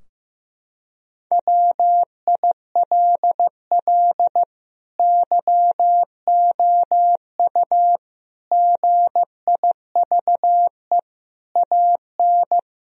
Morse code, heard as a keyed tone, about 15 wpm, 700 Hz. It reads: WILL YOU GIVE AN
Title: Sentences from Top 1000 Words - Review Part 3 - 15wpm